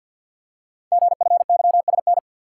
Morse code, heard as a keyed tone, about 46 wpm, 700 Hz. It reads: GFYSN
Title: Code Groups - 5 Letter:Numbers - 50wpm